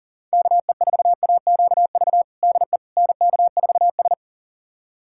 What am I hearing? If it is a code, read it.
Morse code, heard as a keyed tone, about 37 wpm, 700 Hz. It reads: KE4AQV DE NK4S
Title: USA Callsign DE Callsign 40wpm